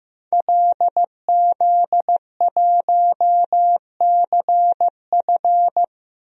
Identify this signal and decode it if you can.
Morse code, heard as a keyed tone, about 15 wpm, 700 Hz. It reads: LZ1CF
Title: DXCC-100 Call Signs - 5 Characters - 15wpm